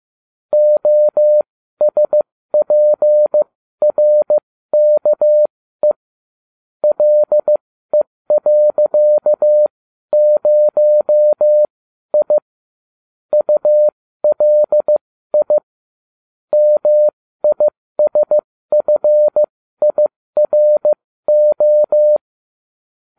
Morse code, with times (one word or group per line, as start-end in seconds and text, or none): 0.5-5.9 OSPRKE
6.8-12.4 LE.0I
13.3-15.6 ULI
16.5-22.2 MISFIRO